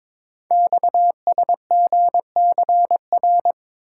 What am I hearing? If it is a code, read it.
Morse code, heard as a keyed tone, about 22 wpm, 700 Hz. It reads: XSGCR
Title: Code Groups - 5 Letters - 22wpm